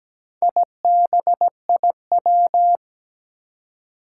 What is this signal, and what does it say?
Morse code, heard as a keyed tone, about 17 wpm, 700 Hz. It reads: IBIW
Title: Four Challenging Letter-Number Combinations 17wpm